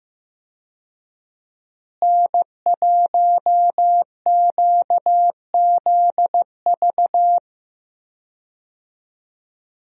Morse code, heard as a keyed tone, about 16 wpm, 700 Hz. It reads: N1QZV
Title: USA Call Signs - 5 Characters - Repeated 2x - 15wpm